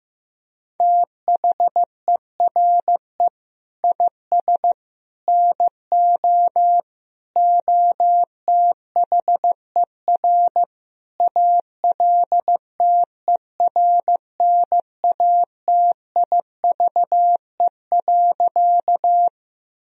Morse code, heard as a keyed tone, about 16 wpm, 700 Hz. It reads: THERE IS NO OTHER ALTERNATIVE.
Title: Sentences from Top 1000 Words - Review Part 4 - 15wpm